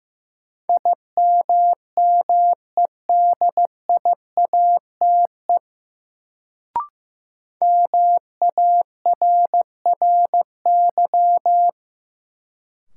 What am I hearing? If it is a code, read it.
Morse code, heard as a keyed tone, about 15 wpm, 700 Hz. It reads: IMMEDIATE MARRY
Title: Sentences from Top 1200 Words - New Word Drill - 15wpm